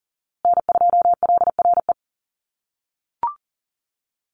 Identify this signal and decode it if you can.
Morse code, heard as a keyed tone, about 40 wpm, 700 Hz. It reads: N2LRE
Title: USA Call Signs Repeated 3x 40wpm